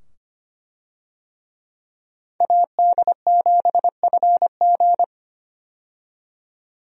Morse code, heard as a keyed tone, about 25 wpm, 700 Hz. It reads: AD7FG
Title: USA Call Signs - 5 Characters - 25wpm